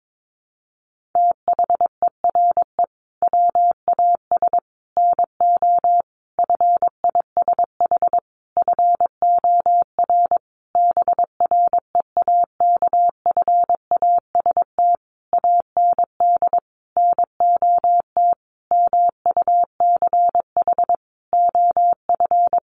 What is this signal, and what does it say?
Morse code, heard as a keyed tone, about 22 wpm, 700 Hz. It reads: THERE WAS NO FISH FOR BREAKFAST AND NOT MUCH OF
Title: Sentences from Top 1500 Words 22wpm